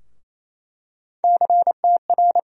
Morse code, heard as a keyed tone, about 28 wpm, 700 Hz. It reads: CTR
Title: USA Call Sign Postfixes - 3 Letters 28wpm